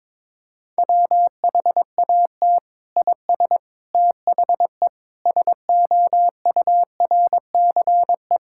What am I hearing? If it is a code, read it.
Morse code, heard as a keyed tone, about 22 wpm, 700 Hz. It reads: WHAT IS THE SOURCE
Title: Sentences from Top 1000 Words - Review Part 5 - 22wpm